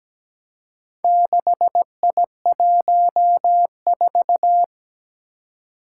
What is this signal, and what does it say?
Morse code, heard as a keyed tone, about 17 wpm, 700 Hz. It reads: 6I14
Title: Four Challenging Letter-Number Combinations 17wpm